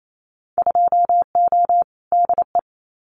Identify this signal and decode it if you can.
Morse code, heard as a keyed tone, about 28 wpm, 700 Hz. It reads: 2O DE
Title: Winter Field Day Exchanges Encore 28wpm